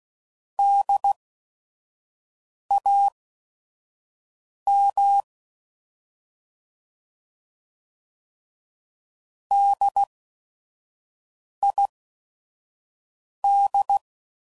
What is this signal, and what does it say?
Morse code, heard as a keyed tone, about 16 wpm, spaced as for 5 wpm, 800 Hz. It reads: DAM DID
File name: Prefixes / By First Letter / Ds